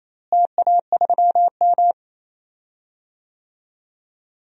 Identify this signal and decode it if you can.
Morse code, heard as a keyed tone, about 28 wpm, 700 Hz. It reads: TA3M
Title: DXCC-100 Call Signs - 4 Characters - Repeated 3x - 28wpm